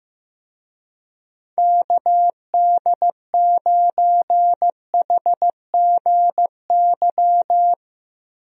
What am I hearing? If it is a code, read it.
Morse code, heard as a keyed tone, about 15 wpm, 700 Hz. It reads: KD9HGY
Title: USA Call Signs - 6 Characters - Repeated 2x - 15wpm